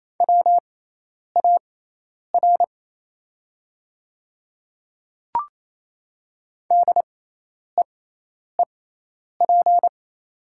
Morse code, heard as a keyed tone, about 28 wpm, spaced as for 10 wpm, 690 Hz. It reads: WAR DEEP
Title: 6x Character Spacing - Top 500 Words - 28wpm